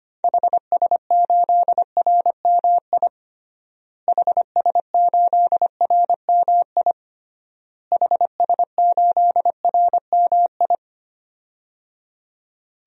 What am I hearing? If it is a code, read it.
Morse code, heard as a keyed tone, about 25 wpm, 700 Hz. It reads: HS8RMI HS8RMI HS8RMI
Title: DXCC-100 Call Signs - 6 Characters - Repeated 3x - 25wpm